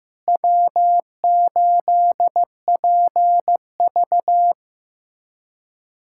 Morse code, heard as a keyed tone, about 16 wpm, 700 Hz. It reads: W8PV
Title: USA Callsign DE Callsign - 4 Characters - 15wpm